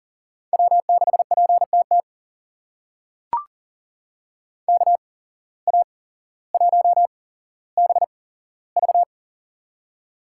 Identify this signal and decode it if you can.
Morse code, heard as a keyed tone, about 40 wpm, spaced as for 17 wpm, 700 Hz. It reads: W6PTT KA1BV
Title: USA Call Signs - 5 Characters - 8x Character Spacing - 40wpm